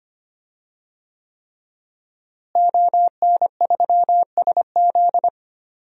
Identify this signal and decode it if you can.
Morse code, heard as a keyed tone, about 25 wpm, 700 Hz. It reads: ON3SZ
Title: DXCC-100 Call Signs - 5 Characters - Repeated 3x - 25wpm